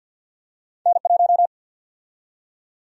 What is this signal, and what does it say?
Morse code, heard as a keyed tone, about 48 wpm, 700 Hz. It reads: N1
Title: USA Call Sign Prefixes - 50wpm